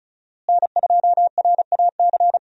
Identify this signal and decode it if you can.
Morse code, heard as a keyed tone, about 36 wpm, 700 Hz. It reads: N2RAC
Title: USA Call Signs - 5 Characters - Repeated 3x - 35wpm